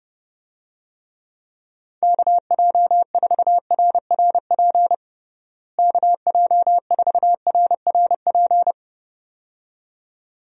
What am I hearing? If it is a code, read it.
Morse code, heard as a keyed tone, about 32 wpm, 700 Hz. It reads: KJ4RRP KJ4RRP
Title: USA Call Signs - 6 Characters - Repeated 2x - 30wpm